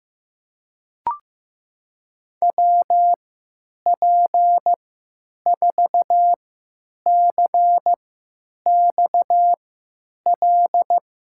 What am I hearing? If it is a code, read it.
Morse code, heard as a keyed tone, about 15 wpm, spaced as for 9 wpm, 700 Hz. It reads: WP4CXL
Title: USA Call Signs - 6 Characters - 3x Character Spacing - 15wpm